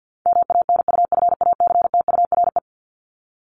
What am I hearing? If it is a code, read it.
Morse code, heard as a keyed tone, about 48 wpm, 700 Hz. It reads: MANUFACTURE
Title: ING Suffix - Top 100 Words with Base Words 50wpm